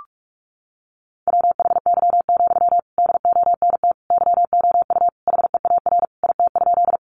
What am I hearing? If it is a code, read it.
Morse code, heard as a keyed tone, about 44 wpm, 700 Hz. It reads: WHY, DONT YOU HEAR IT?
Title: Sentences from Top 300 Words 45wpm